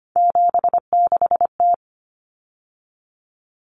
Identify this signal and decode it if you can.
Morse code, heard as a keyed tone, about 25 wpm, 700 Hz. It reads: 76T
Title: Common Cut Numbers - 3-Digit 25wpm